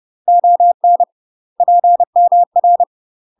Morse code, heard as a keyed tone, about 30 wpm, 700 Hz. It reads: ON PMR
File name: UK MASTER QSOs TRAINING / 30WPM / UK MASTER QSOs 0032